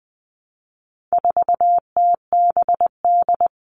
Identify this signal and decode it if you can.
Morse code, heard as a keyed tone, about 20 wpm, 700 Hz. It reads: 4TBD
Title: All Cut Numbers - 4-Digit 20wpm